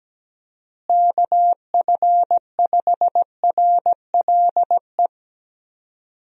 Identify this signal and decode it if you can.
Morse code, heard as a keyed tone, about 17 wpm, 700 Hz. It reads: KF5RLE
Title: USA Call Signs - 6 Characters - Repeated 3x - 17wpm